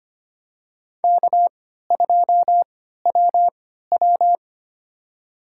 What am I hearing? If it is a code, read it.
Morse code, heard as a keyed tone, about 25 wpm, spaced as for 14 wpm, 700 Hz. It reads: K2WW
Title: USA Call Signs - 4 Characters - 3x Character Spacing - 25wpm